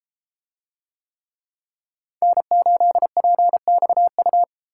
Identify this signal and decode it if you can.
Morse code, heard as a keyed tone, about 28 wpm, 700 Hz. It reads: N8PXU